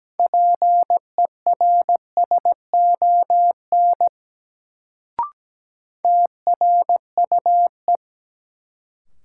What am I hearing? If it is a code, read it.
Morse code, heard as a keyed tone, about 17 wpm, 700 Hz. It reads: PERSON TRUE
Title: Sentences from Top 400 Words - New Word Drill - 17wpm